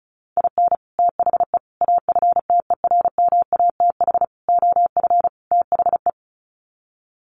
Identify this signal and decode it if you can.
Morse code, heard as a keyed tone, about 34 wpm, 700 Hz. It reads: IN THE AFTERMATH OF THE
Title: Sets of 5 Words 35wpm